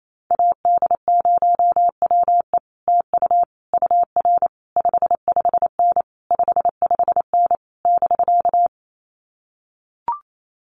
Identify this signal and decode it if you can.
Morse code, heard as a keyed tone, about 28 wpm, 700 Hz. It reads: AD0WE TU UR 55N 55N <BK>
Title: Common QSO Phrases Encore 28wpm